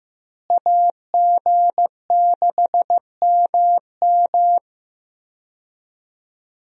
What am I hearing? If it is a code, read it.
Morse code, heard as a keyed tone, about 15 wpm, 700 Hz. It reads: AG6MM
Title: USA Call Signs - 5 Characters - Repeated 3x - 15wpm